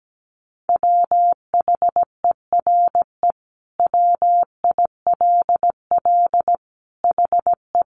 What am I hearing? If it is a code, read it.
Morse code, heard as a keyed tone, about 17 wpm, 700 Hz. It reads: WHERE WILL HE
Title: Sentences from Top 1000 Words 17wpm